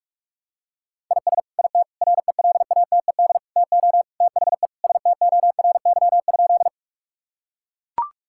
Morse code, heard as a keyed tone, about 41 wpm, 700 Hz. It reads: IS IT RELATED TO THE STORY?
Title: Sentences from Top 1000 Words - Review Part 5 - 45wpm